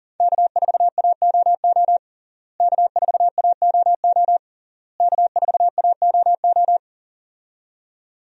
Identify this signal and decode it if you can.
Morse code, heard as a keyed tone, about 40 wpm, 700 Hz. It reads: K4AOO K4AOO K4AOO
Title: USA Call Signs - 5 Characters - Repeated 3x - 40wpm